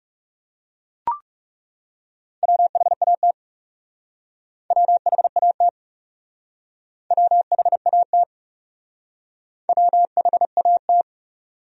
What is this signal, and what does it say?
Morse code, heard as a keyed tone, about 37 wpm, 700 Hz. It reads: WHAT WHAT WHAT WHAT
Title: Speed-Racing Top 100 Words 30wpm